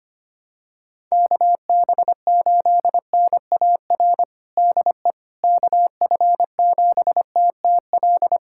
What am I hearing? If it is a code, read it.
Morse code, heard as a keyed tone, about 25 wpm, 700 Hz. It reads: KB8NAR DE KF7TTL
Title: USA Callsign DE Callsign - 6 Characters - 25wpm